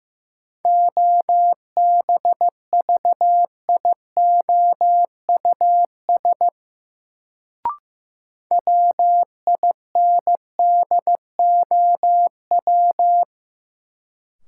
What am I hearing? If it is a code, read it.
Morse code, heard as a keyed tone, about 15 wpm, 700 Hz. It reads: OBVIOUS WINDOW
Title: Sentences from Top 1600 Words - New Word Drill Alt - 15wpm